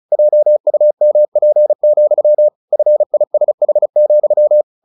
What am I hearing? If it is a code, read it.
Morse code, heard as a keyed tone, about 34 wpm, 600 Hz. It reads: JUMP, FISH,
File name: the old man and the sea 2 35wpm